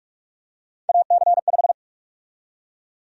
Morse code, heard as a keyed tone, about 46 wpm, 700 Hz. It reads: AC5
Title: USA Call Sign Prefixes - 45wpm